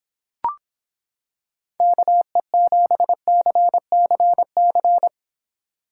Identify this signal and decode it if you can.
Morse code, heard as a keyed tone, about 26 wpm, 700 Hz. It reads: KE7CCC